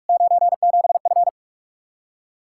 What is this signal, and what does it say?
Morse code, heard as a keyed tone, about 60 wpm, spaced as for 53 wpm, 700 Hz. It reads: 97F
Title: Speed-Racing Common QSO Phrases Encore - 45wpm